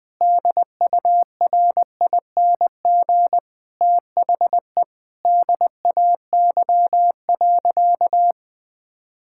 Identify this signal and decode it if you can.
Morse code, heard as a keyed tone, about 21 wpm, 700 Hz. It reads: DURING THE DAY.
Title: Sentences from Top 1000 Words - Review Part 6 - 20wpm